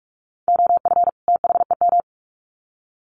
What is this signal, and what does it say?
Morse code, heard as a keyed tone, about 44 wpm, 700 Hz. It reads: OF THEM
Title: Sets of 2 Words 45wpm